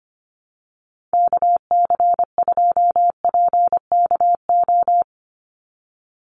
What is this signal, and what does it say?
Morse code, heard as a keyed tone, about 25 wpm, 700 Hz. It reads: KC2PKO
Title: USA Call Signs 25wpm